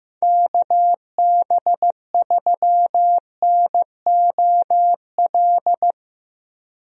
Morse code, heard as a keyed tone, about 15 wpm, 700 Hz. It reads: KB3NOL
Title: USA Call Signs - 6 Characters - Repeated 2x - 15wpm